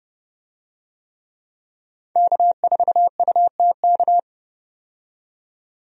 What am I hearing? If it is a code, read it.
Morse code, heard as a keyed tone, about 30 wpm, 700 Hz. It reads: K4UTK
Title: USA Call Signs - 5 Characters - 30wpm